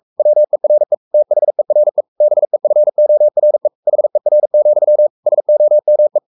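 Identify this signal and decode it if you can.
Morse code, heard as a keyed tone, about 44 wpm, 600 Hz. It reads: WERE THERE BEFORE HER, SOME
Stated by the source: Morse code at 43 wpm sending Ken Follett's novel Never in English, from CW med Gnister.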